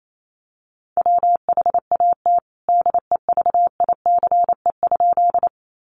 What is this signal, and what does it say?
Morse code, heard as a keyed tone, about 28 wpm, 700 Hz. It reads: WHAT DEVICE?
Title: Sentences from Top 1500 Words 28wpm